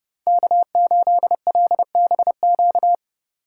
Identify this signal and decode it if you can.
Morse code, heard as a keyed tone, about 28 wpm, 700 Hz. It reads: K8LBQ